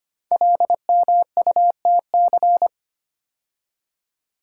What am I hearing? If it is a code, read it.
Morse code, heard as a keyed tone, about 25 wpm, 700 Hz. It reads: LMUTC
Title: Code Groups - 5 Letters - 25wpm